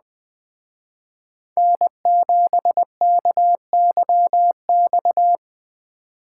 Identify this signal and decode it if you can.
Morse code, heard as a keyed tone, about 20 wpm, 700 Hz. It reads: N7KYX
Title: Speed Racing - USA Call Signs 17wpm